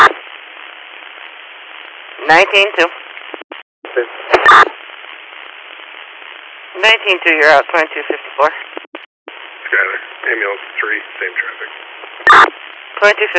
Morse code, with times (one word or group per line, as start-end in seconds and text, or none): none